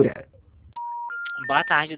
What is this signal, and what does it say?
Special information tone, with speech.